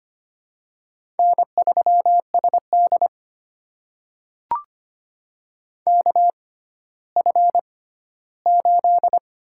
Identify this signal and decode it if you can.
Morse code, heard as a keyed tone, about 25 wpm, spaced as for 14 wpm, 700 Hz. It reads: N3SD KF8
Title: USA Call Signs - 4 Characters - 6x Character Spacing - 25wpm